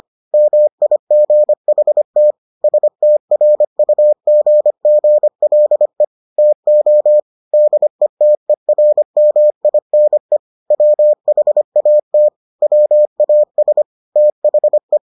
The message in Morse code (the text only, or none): MIGHT STRUGGLE TO DETERMINE WHAT WAS THE